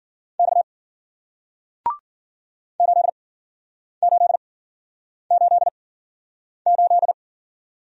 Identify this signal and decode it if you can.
Morse code, heard as a keyed tone, about 55 wpm, spaced as for 9 wpm, 700 Hz. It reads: X 8888